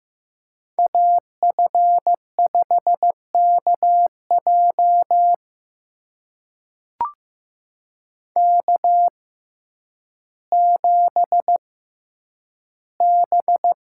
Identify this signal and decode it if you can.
Morse code, heard as a keyed tone, about 15 wpm, spaced as for 9 wpm, 700 Hz. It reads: AF5KJ K7B
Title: USA Call Signs - 5 Characters - 6x Character Spacing - 15wpm